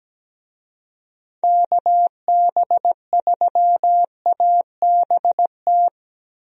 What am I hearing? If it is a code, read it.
Morse code, heard as a keyed tone, about 17 wpm, 700 Hz. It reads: KB3ABT
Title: USA Call Signs - 6 Characters - 17wpm